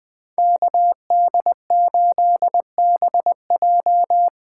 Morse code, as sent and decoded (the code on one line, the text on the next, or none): -.- -.. ---.. -... .---
KD8BJ